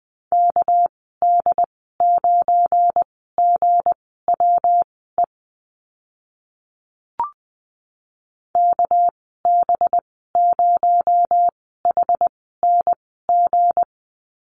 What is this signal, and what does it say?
Morse code, heard as a keyed tone, about 20 wpm, spaced as for 14 wpm, 700 Hz. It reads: KD9GWE KB0HNG
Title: 238 Lesson 38g - Letter q - Call Signs 20wpm